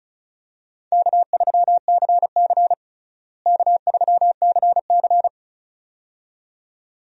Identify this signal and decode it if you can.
Morse code, heard as a keyed tone, about 36 wpm, 700 Hz. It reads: K3CC K3CC
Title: USA Call Signs - 4 Characters - Repeated 2x - 35wpm